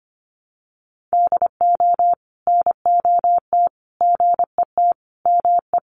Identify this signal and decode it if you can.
Morse code, heard as a keyed tone, about 25 wpm, 700 Hz. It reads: DO NOT GET ME
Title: Sets of 5 Words Encore 25wpm